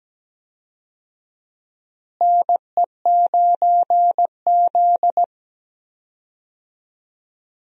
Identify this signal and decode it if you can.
Morse code, heard as a keyed tone, about 17 wpm, 700 Hz. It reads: NE9Z
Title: USA Call Signs - 4 Characters - Repeated 2x - 17wpm